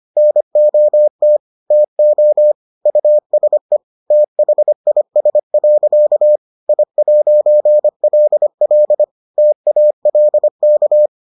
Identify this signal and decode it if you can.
Morse code, heard as a keyed tone, about 25 wpm, 600 Hz. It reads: NOT TO USE THIS. I'LL TALK